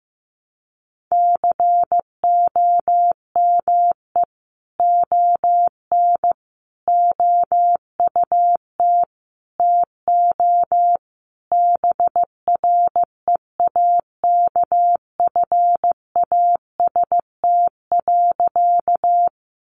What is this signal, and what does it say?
Morse code, heard as a keyed tone, about 15 wpm, 700 Hz. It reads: COME ON OUT TO BREAKFAST.
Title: Sentences from Top 1500 Words 15wpm